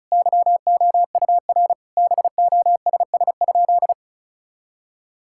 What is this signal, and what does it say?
Morse code, heard as a keyed tone, about 36 wpm, 700 Hz. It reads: YOUR BOSS?
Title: Sentences from Top 1000 Words - Review Part 6 - 35wpm